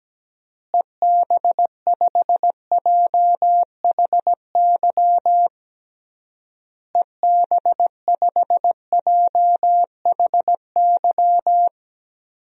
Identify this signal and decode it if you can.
Morse code, heard as a keyed tone, about 17 wpm, 700 Hz. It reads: EB5JHY EB5JHY
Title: DXCC-100 Call Signs - 6 Characters - Repeated 3x - 17wpm